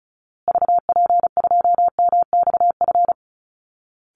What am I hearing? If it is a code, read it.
Morse code, heard as a keyed tone, about 34 wpm, 700 Hz. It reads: VP2MXF
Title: Call Signs Repeated 3x - 35wpm